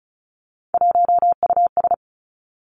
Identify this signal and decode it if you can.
Morse code, heard as a keyed tone, about 36 wpm, 700 Hz. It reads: 1US
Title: Three Challenging Letter-Number Combinations 35wpm